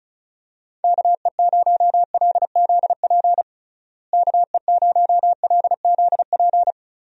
Morse code, heard as a keyed tone, about 36 wpm, 700 Hz. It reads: KE0LZP KE0LZP
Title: USA Call Signs - 6 Characters - Repeated 3x - 35wpm